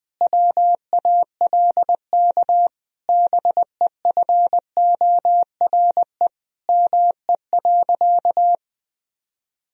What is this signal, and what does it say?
Morse code, heard as a keyed tone, about 21 wpm, 700 Hz. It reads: WALK BEFORE ME.
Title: Sentences from Top 1000 Words - Review Part 1 - 20wpm